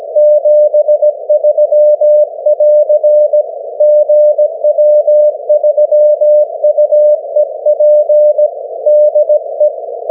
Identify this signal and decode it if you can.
Morse code, heard as a keyed tone, about 17 wpm, 600 Hz. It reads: 73<AR> GW3UEP DE
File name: GW3UEP / 16Dec12 9AM G3XIZ 501kHz